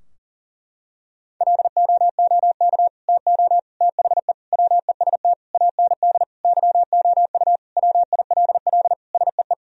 Morse code, heard as a keyed tone, about 40 wpm, 700 Hz. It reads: LOOK TO THE WEST AND YOU WILL SEE
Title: Sentences from Top 1000 Words - Review Part 2 - 40wpm